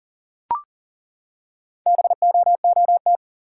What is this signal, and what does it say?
Morse code, heard as a keyed tone, about 42 wpm, 700 Hz. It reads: BOOT
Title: Four Letter Words - 40wpm